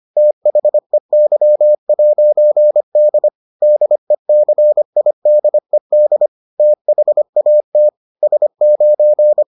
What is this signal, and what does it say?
Morse code, heard as a keyed tone, about 25 wpm, 600 Hz. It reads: THEY'D DECIDED THAT S9